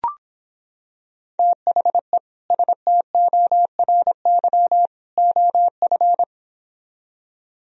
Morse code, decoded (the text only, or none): THE STORY OF